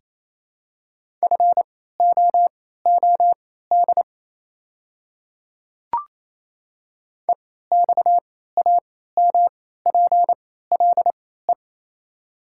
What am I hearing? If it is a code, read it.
Morse code, heard as a keyed tone, about 28 wpm, spaced as for 16 wpm, 700 Hz. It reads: FOOD EXAMPLE
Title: 3x Character Spacing - Top 300 Words - 28wpm